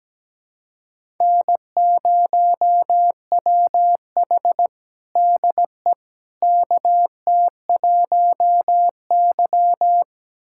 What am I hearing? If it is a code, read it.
Morse code, heard as a keyed tone, about 17 wpm, 700 Hz. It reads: N0WH DE KT1Y
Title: USA Callsign DE Callsign - 4 Characters - 17wpm